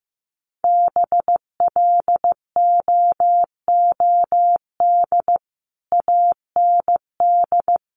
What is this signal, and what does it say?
Morse code, heard as a keyed tone, about 15 wpm, 700 Hz. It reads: BLOOD AND
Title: Sentences from Top 1100 Words 15wpm